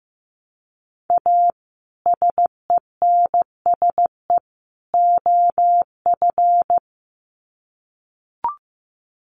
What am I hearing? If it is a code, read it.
Morse code, heard as a keyed tone, about 15 wpm, 700 Hz. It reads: A SENSE OF
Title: Sets of 3 Words 15wpm